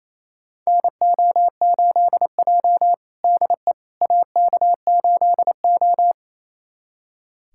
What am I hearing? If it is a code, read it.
Morse code, heard as a keyed tone, about 28 wpm, 700 Hz. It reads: NO8J DE AK8O
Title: USA Callsign DE Callsign - 4 Characters - 28wpm